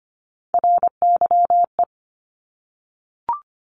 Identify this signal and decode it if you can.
Morse code, heard as a keyed tone, about 25 wpm, 700 Hz. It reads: RYE
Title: Multi-letter Phonemes Repeated 3x 25wpm